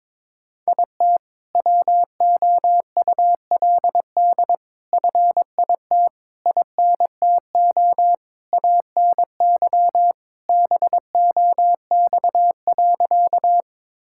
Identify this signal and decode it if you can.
Morse code, heard as a keyed tone, about 22 wpm, 700 Hz. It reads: IT WOULD FIT INTO ANY BOX.
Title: Sentences from Top 1000 Words - Review Part 5 - 22wpm